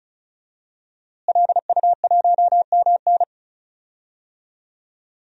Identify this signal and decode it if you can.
Morse code, heard as a keyed tone, about 35 wpm, 700 Hz. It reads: LU1MN
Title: DXCC-100 Call Signs - 5 Characters - Repeated 2x - 35wpm